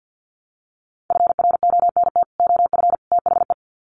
Morse code, heard as a keyed tone, about 50 wpm, 700 Hz. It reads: FRONT OF THE
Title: Sets of 3 Words 50wpm